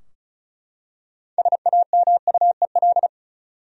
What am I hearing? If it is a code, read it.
Morse code, heard as a keyed tone, about 36 wpm, 700 Hz. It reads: SAMUEL
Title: USA Common Names - 35wpm